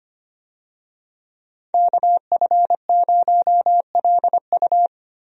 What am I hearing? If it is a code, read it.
Morse code, heard as a keyed tone, about 25 wpm, 700 Hz. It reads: KF0LU
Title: USA Call Signs - 5 Characters - Repeated 3x - 25wpm